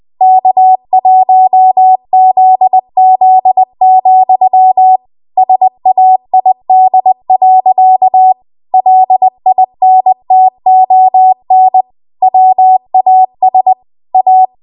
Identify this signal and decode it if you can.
Morse code, heard as a keyed tone, about 20 wpm, 750 Hz. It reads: K1ZZ, SAID. LINTON WAS A
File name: Archive / 20 WPM / 20WPM